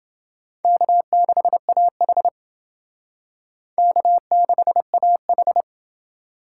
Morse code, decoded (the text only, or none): K6AH K6AH